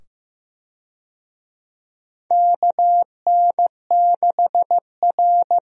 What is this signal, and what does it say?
Morse code, heard as a keyed tone, about 16 wpm, 700 Hz. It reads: KN6R
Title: USA Call Signs - 4 Characters - 15wpm